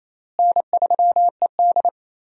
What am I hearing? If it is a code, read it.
Morse code, heard as a keyed tone, about 28 wpm, 700 Hz. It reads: N3ED
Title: USA Call Signs - 4 Characters - Repeated 3x - 28wpm